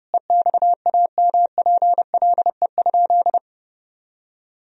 Morse code, heard as a keyed tone, about 30 wpm, 700 Hz. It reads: EXAMPLE?
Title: Sentences from Top 1000 Words - Review Part 1 - 30wpm